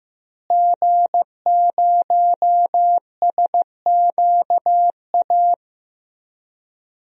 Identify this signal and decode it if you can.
Morse code, heard as a keyed tone, about 15 wpm, 700 Hz. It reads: G0SQA
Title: DXCC-100 Call Signs - 5 Characters - Repeated 2x - 15wpm